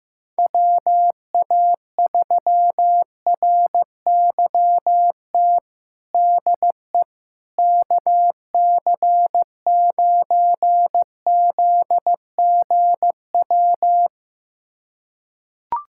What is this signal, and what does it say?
Morse code, heard as a keyed tone, about 15 wpm, 700 Hz. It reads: WA3RYT DE KC9ZGW
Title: USA Callsign DE Callsign 15wpm